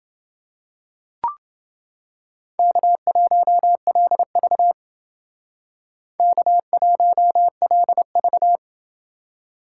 Morse code, heard as a keyed tone, about 29 wpm, 700 Hz. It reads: K1LV K1LV